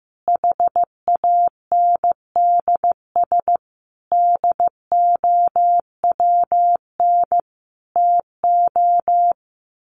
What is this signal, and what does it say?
Morse code, heard as a keyed tone, about 15 wpm, 700 Hz. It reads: HANDS DOWN TO
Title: A Sound Of Thunder 15wpm